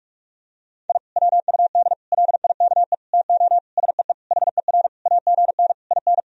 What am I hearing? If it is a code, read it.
Morse code, heard as a keyed tone, about 42 wpm, 700 Hz. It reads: I WUD LIKE TO SEE HER AGN IN
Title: Sentences from Top 200 Words and Abbreviations 45wpm